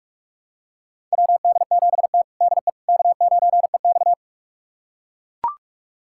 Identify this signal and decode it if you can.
Morse code, heard as a keyed tone, about 43 wpm, 700 Hz. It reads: WD7T DE K9EX
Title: USA Callsign DE Callsign - 4 Characters - 45wpm